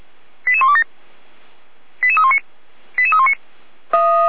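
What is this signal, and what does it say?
Five-tone selective call (ZVEI), heard as quick runs of five tones, tones of about 70 ms. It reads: 88127 88319 88319
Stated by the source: ZVEI five-tone selective call received on SDR.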